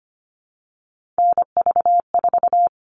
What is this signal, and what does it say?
Morse code, heard as a keyed tone, about 25 wpm, 700 Hz. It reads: NV4
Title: All Cut Numbers - 3-Digit 25wpm